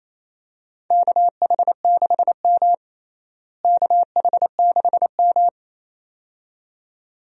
Morse code, heard as a keyed tone, about 28 wpm, 700 Hz. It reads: KH6M KH6M
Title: DXCC-100 Call Signs - 4 Characters - Repeated 2x - 28wpm